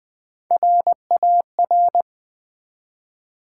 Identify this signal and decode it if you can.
Morse code, heard as a keyed tone, about 20 wpm, 700 Hz. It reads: RAR
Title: Build A Word - 1 Letter At A Time - 4-Letter Words - 20wpm